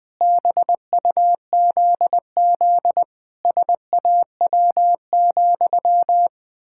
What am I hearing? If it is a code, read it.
Morse code, heard as a keyed tone, about 21 wpm, 700 Hz. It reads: BUZZ SAW,
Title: Pangram Challenge v2 - 20wpm